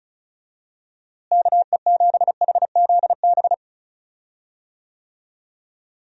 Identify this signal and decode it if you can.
Morse code, heard as a keyed tone, about 36 wpm, 700 Hz. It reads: KE7HZB
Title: USA Call Signs - 6 Characters - Repeated 2x - 35wpm